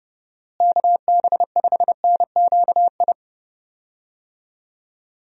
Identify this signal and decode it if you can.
Morse code, heard as a keyed tone, about 32 wpm, 700 Hz. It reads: KB5NQI
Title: USA Call Signs - 6 Characters - Repeated 2x - 30wpm